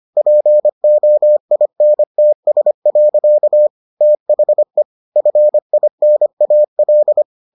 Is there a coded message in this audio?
Yes: Morse code from 0.2 to 7.2 s, heard as a keyed tone, about 25 wpm, 600 Hz. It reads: POINTS. THE FINAL